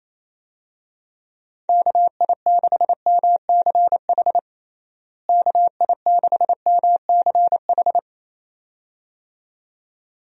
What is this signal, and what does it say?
Morse code, heard as a keyed tone, about 27 wpm, 700 Hz. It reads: KI6MCH KI6MCH
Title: USA Call Signs - 6 Characters - Repeated 2x - 28wpm